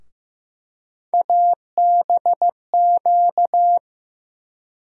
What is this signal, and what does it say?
Morse code, heard as a keyed tone, about 15 wpm, 700 Hz. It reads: ABQ
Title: USA Call Sign Postfixes - 3 Letters 15wpm